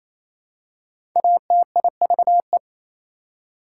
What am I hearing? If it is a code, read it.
Morse code, heard as a keyed tone, about 28 wpm, 700 Hz. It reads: ATIVE
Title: Prefixes and Suffixes 28wpm